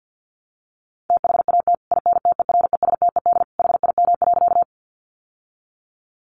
Morse code, heard as a keyed tone, about 50 wpm, 700 Hz. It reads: THAT INTERESTED HIM.
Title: Sentences from Top 900 Words 50wpm